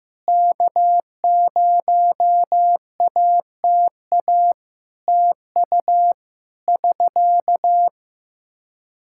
Morse code, heard as a keyed tone, about 14 wpm, 700 Hz. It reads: K0ATA TU <SK>
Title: Practice QSO - Exchange 4 - Ending - 15wpm